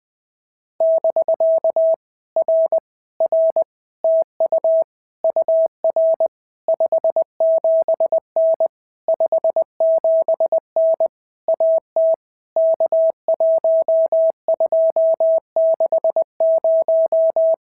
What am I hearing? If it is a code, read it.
Morse code, heard as a keyed tone, about 20 wpm, 650 Hz. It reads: <BK> R R TU UR 57N 57N AT K1260
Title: POTA QSO Practice - 20wpm